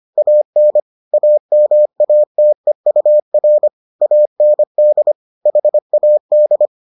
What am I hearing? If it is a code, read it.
Morse code, heard as a keyed tone, about 25 wpm, 600 Hz. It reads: AN AMATEUR AND HAD